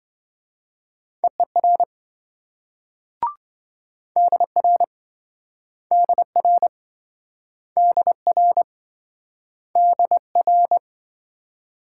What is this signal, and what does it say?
Morse code, heard as a keyed tone, about 27 wpm, 700 Hz. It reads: EER DR DR DR DR